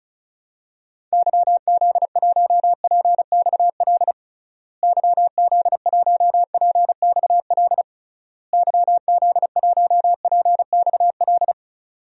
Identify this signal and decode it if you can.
Morse code, heard as a keyed tone, about 36 wpm, 700 Hz. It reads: YZ1PXL YZ1PXL YZ1PXL
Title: DXCC-100 Call Signs - 6 Characters - Repeated 3x - 35wpm